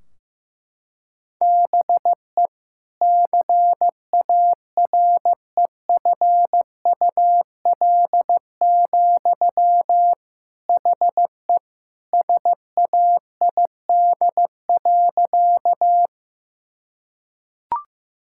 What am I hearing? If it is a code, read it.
Morse code, heard as a keyed tone, about 15 wpm, 700 Hz. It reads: BE CAREFUL, HE SAID.
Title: Sentences from Top 1600 Words 15wpm